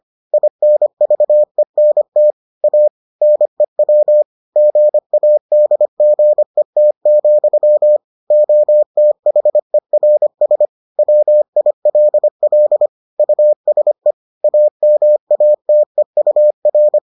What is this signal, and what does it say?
Morse code, heard as a keyed tone, about 25 wpm, 600 Hz. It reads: INVENT A NEW GADGET, OTHERS WILL USE AMATEUR